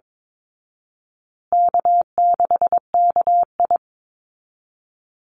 Morse code, heard as a keyed tone, about 22 wpm, 700 Hz. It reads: K6KI
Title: Call Signs Repeated 2x - 22wpm